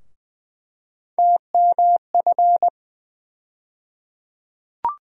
Morse code, heard as a keyed tone, about 20 wpm, 700 Hz. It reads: TMF
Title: USA Call Sign Postfixes - 3 Letters 20wpm